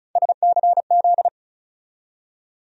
Morse code, heard as a keyed tone, about 34 wpm, 700 Hz. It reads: SCZ
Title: USA Call Sign Postfixes - 3 Letters 35wpm